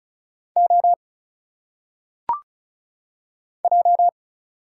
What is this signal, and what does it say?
Morse code, heard as a keyed tone, about 34 wpm, 700 Hz. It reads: O J
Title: Single Letters - Phonetic Pronunciation 35wpm